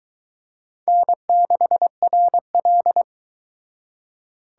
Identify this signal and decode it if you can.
Morse code, heard as a keyed tone, about 22 wpm, 700 Hz. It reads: N6RL